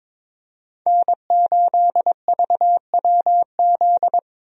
Morse code, heard as a keyed tone, about 22 wpm, 700 Hz. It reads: N8VWZ
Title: USA Call Signs - 5 Characters - Repeated 3x - 22wpm